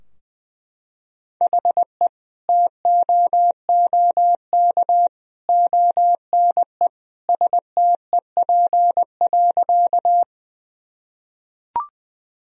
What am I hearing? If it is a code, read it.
Morse code, heard as a keyed tone, about 21 wpm, 700 Hz. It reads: HE TOOK ONE STEP.
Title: Sentences from Top 1000 Words - Review Part 2 - 20wpm